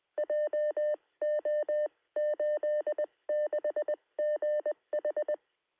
Morse code, heard as a keyed tone, about 20 wpm, spaced as for 17 wpm, 600 Hz. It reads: JO86GH
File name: sm7lcb / 2117z g3xdy 9cm